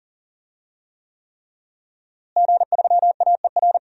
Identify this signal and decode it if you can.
Morse code, heard as a keyed tone, about 40 wpm, 700 Hz. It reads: G3AER
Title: DXCC-100 Call Signs - 5 Characters - Repeated 3x - 40wpm